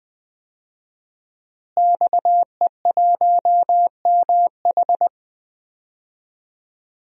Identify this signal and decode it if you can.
Morse code, heard as a keyed tone, about 20 wpm, 700 Hz. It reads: XE1MH
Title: DXCC-100 Call Signs - 5 Characters - Repeated 3x - 20wpm